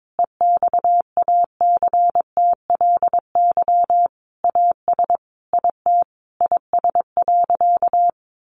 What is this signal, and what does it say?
Morse code, heard as a keyed tone, about 22 wpm, 700 Hz. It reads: EXACTLY AS IT IS.
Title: Sentences from Top 900 Words 22wpm